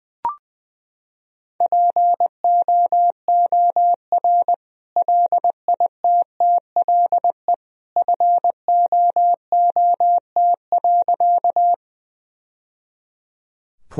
Morse code, heard as a keyed tone, about 21 wpm, 700 Hz. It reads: POOR LITTLE FOOT.
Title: Sentences from Top 1000 Words - Review Part 3 - 20wpm